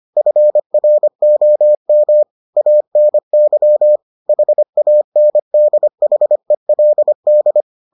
Morse code, heard as a keyed tone, about 25 wpm, 600 Hz. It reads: FROM ANY HANDHELD